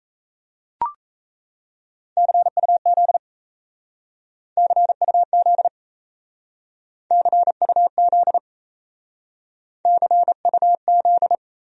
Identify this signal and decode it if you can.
Morse code, heard as a keyed tone, about 34 wpm, 700 Hz. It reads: CUZ CUZ CUZ CUZ
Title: Speed-Racing Abbreviations - 28wpm